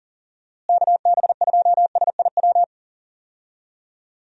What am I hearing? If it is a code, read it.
Morse code, heard as a keyed tone, about 40 wpm, 700 Hz. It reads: KB2SIW
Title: USA Call Signs - 6 Characters - Repeated 3x - 40wpm